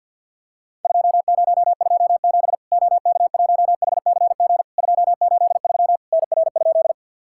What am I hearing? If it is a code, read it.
Morse code, heard as a keyed tone, about 50 wpm, 700 Hz. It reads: 2027 OK1HCG 283 NR?
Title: Worked All Europe DX Contest - QTC Exchange with Repeats - 50wpm